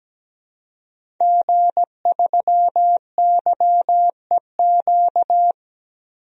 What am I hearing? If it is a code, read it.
Morse code, heard as a keyed tone, about 17 wpm, 700 Hz. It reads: G3YEQ
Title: DXCC-100 Call Signs - 5 Characters - 17wpm